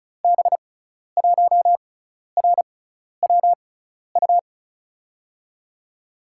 Morse code, heard as a keyed tone, about 36 wpm, spaced as for 12 wpm, 700 Hz. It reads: B1RWU
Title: USA Call Signs - 6 Characters - 6x Character Spacing - 35wpm